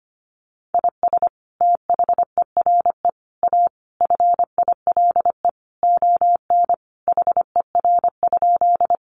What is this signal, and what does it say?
Morse code, heard as a keyed tone, about 25 wpm, 700 Hz. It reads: IS THERE A FILE ON HER?
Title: Sentences from Top 1200 Words 25wpm